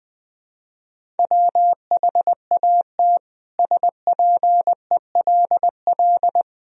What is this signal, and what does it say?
Morse code, heard as a keyed tone, about 20 wpm, 700 Hz. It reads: WHAT SPELL
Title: Sentences from Top 1000 Words - Review Part 1 - 20wpm